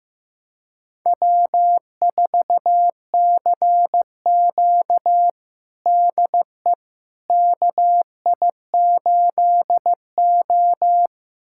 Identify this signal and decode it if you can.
Morse code, heard as a keyed tone, about 15 wpm, 700 Hz. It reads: W4CQ DE KI8O
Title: USA Callsign DE Callsign - 4 Characters - 15wpm